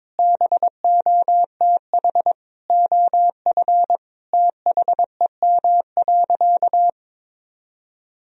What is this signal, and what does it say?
Morse code, heard as a keyed tone, about 22 wpm, 700 Hz. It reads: BOTH OF THEM.
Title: Sentences from Top 1000 Words - Review Part 1 - 22wpm